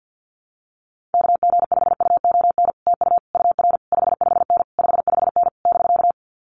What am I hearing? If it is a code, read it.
Morse code, heard as a keyed tone, about 50 wpm, 700 Hz. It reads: KG5UON TU UR 55N 55N <BK>
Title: Common QSO Phrases Encore 50wpm